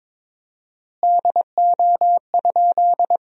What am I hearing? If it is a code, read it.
Morse code, heard as a keyed tone, about 22 wpm, 700 Hz. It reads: DO?
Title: 4x Word-Spacing - Sentences from Top 100 Words - 22wpm